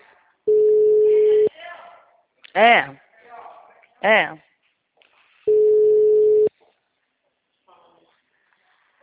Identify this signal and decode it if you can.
Ringback tone (Europe, 425 Hz), with speech.
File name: out- -2018- - - .4901